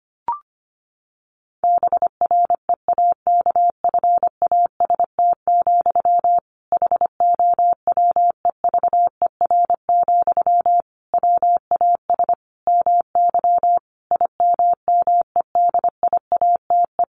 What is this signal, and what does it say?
Morse code, heard as a keyed tone, about 25 wpm, 700 Hz. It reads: BREAKFAST, HOWEVER, WAS MY IMMEDIATE
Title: Sentences from Top 1500 Words 25wpm